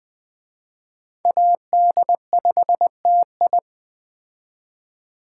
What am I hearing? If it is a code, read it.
Morse code, heard as a keyed tone, about 21 wpm, 700 Hz. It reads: AD5TI
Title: Speed Racing - USA 2x2 Call Signs 17wpm